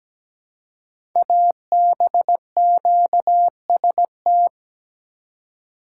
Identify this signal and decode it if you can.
Morse code, heard as a keyed tone, about 17 wpm, 700 Hz. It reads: ABQST
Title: Code Groups - 5 Letters - 17wpm